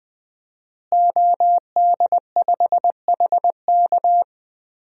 Morse code, heard as a keyed tone, about 20 wpm, 700 Hz. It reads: OD5HK
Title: DXCC-100 Call Signs - 5 Characters - Repeated 3x - 20wpm